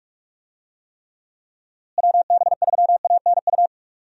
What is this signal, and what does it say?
Morse code, heard as a keyed tone, about 46 wpm, 700 Hz. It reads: WB3ANU
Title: USA Call Signs - 6 Characters - Repeated 2x - 45wpm